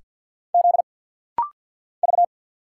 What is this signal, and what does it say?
Morse code, heard as a keyed tone, about 50 wpm, 700 Hz. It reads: Z V
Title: Single Letters - Rapid-Fire 50wpm